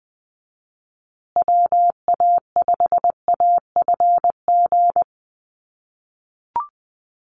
Morse code, heard as a keyed tone, about 20 wpm, 700 Hz. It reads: WA5AFG
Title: USA Call Signs Repeated 2x 20wpm